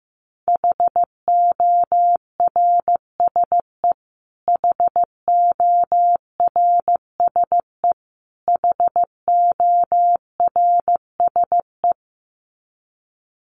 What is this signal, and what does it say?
Morse code, heard as a keyed tone, about 15 wpm, 700 Hz. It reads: HORSE HORSE HORSE
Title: Top 500 Words Repeated 15wpm